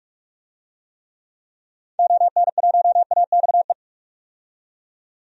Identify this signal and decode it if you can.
Morse code, heard as a keyed tone, about 42 wpm, 700 Hz. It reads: ON1AXE
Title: DXCC-100 Call Signs - 6 Characters - 45wpm